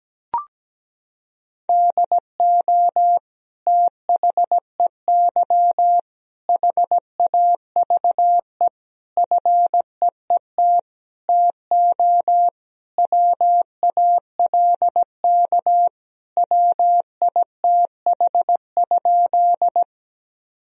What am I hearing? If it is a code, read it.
Morse code, heard as a keyed tone, about 17 wpm, 700 Hz. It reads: DO THEY HAVE FEET TO WALK WITH?
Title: Sentences from Top 1000 Words - Review Part 1 - 17wpm